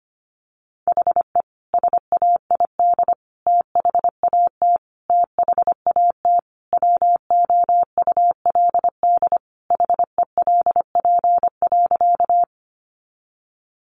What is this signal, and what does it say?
Morse code, heard as a keyed tone, about 25 wpm, 700 Hz. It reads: HE SAID THAT THAT WOULD HELP.
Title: Sentences from Top 200 Words 25wpm